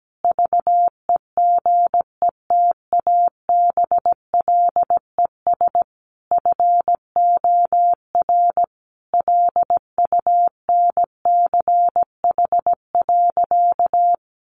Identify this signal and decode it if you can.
Morse code, heard as a keyed tone, about 17 wpm, 700 Hz. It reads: VEGETABLES FOR LUNCH.
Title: Easy Phrases 17wpm